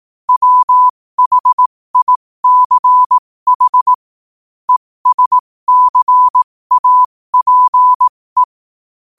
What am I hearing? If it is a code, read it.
Morse code, heard as a keyed tone, about 18 wpm, spaced as for 15 wpm, 1000 Hz. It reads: WHICH ESCAPE